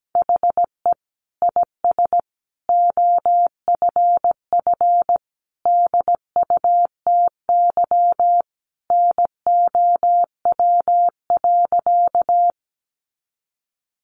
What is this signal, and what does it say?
Morse code, heard as a keyed tone, about 17 wpm, 700 Hz. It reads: HE IS OFF DUTY NOW.
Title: Sentences from Top 1500 Words 17wpm